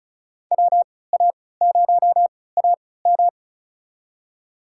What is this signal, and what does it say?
Morse code, heard as a keyed tone, about 36 wpm, spaced as for 20 wpm, 690 Hz. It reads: WA0AM
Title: USA Call Signs - 5 Characters - 3x Character Spacing - 35wpm